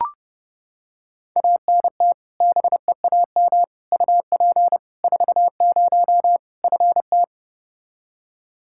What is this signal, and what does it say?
Morse code, heard as a keyed tone, about 30 wpm, 700 Hz. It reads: ANT BEAM UP 40 FT
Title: Practice QSO - Exchange 2 - Info - 30wpm